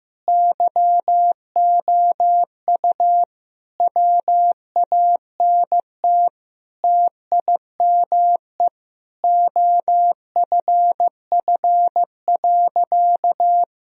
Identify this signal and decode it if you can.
Morse code, heard as a keyed tone, about 15 wpm, 700 Hz. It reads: YOU WANT TIME OFF.
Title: Sentences from Top 1000 Words - Review Part 3 - 15wpm